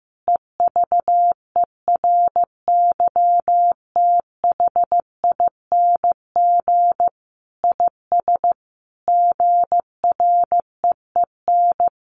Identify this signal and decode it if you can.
Morse code, heard as a keyed tone, about 14 wpm, 700 Hz. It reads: EVERYTHING IS GREEN
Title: Sentences from Top 700 Words 15wpm